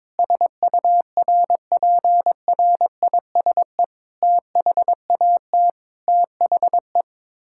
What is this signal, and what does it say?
Morse code, heard as a keyed tone, about 22 wpm, 700 Hz. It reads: SURPRISE THAT THE
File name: practice / 22wpm / Sets of 8 Words - 22wpm